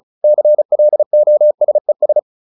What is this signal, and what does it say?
Morse code, heard as a keyed tone, about 34 wpm, 600 Hz. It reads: CLOSES